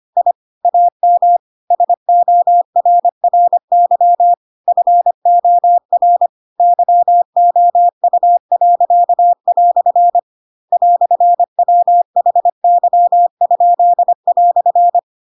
Morse code, heard as a keyed tone, about 25 wpm, 700 Hz. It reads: I AM SORRY FOR YOU." "WHY?"
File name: Pinocchio / Pinocchio ch4 25 25 ik1hge